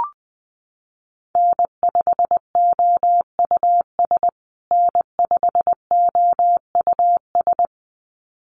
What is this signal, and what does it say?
Morse code, heard as a keyed tone, about 20 wpm, 700 Hz. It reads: N5OUS N5OUS
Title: USA Call Signs Repeated 2x 20wpm